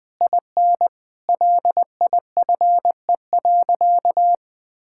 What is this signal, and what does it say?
Morse code, heard as a keyed tone, about 21 wpm, 700 Hz. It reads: IN LIFE.
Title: Sentences from Top 1000 Words - Review Part 6 - 20wpm